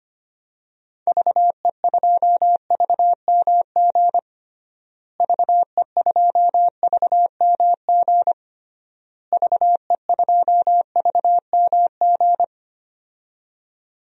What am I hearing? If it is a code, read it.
Morse code, heard as a keyed tone, about 25 wpm, 700 Hz. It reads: VE2VMG VE2VMG VE2VMG
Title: DXCC-100 Call Signs - 6 Characters - Repeated 3x - 25wpm